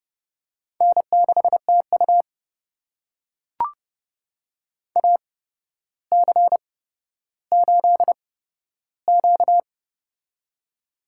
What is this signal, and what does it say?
Morse code, heard as a keyed tone, about 29 wpm, spaced as for 13 wpm, 700 Hz. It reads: N6TU AC8Q
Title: USA Call Signs - 4 Characters - 8x Character Spacing - 30wpm